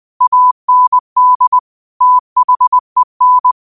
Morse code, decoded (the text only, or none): AND THEN